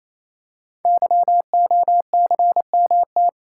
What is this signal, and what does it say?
Morse code, heard as a keyed tone, about 28 wpm, 700 Hz. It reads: YOCMT
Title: Code Groups - 5 Letters - 28wpm